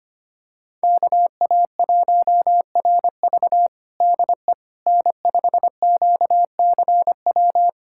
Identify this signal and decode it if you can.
Morse code, heard as a keyed tone, about 25 wpm, 700 Hz. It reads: KA1RV DE N5QCW
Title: USA Callsign DE Callsign - 5 Characters - 25wpm